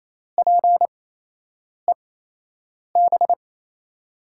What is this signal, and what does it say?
Morse code, heard as a keyed tone, about 28 wpm, spaced as for 8 wpm, 700 Hz. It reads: PEB
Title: USA Call Signs - 6 Characters - 8x Character Spacing - 28wpm